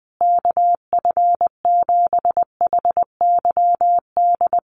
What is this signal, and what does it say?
Morse code, heard as a keyed tone, about 20 wpm, 700 Hz. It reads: KF7HYD
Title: USA Call Signs Repeated 2x 20wpm